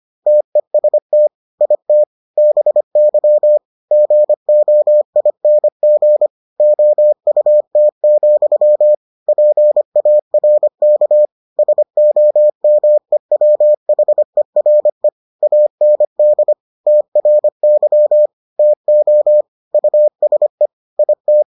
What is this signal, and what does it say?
Morse code, heard as a keyed tone, about 25 wpm, 600 Hz. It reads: TEST IT BY GOING OUT, PARK SOMEWHERE AND TRY TO USE IT